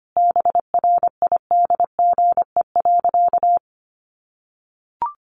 Morse code, heard as a keyed tone, about 25 wpm, 700 Hz. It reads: BRIDGE.
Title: Sentences from Top 1200 Words 25wpm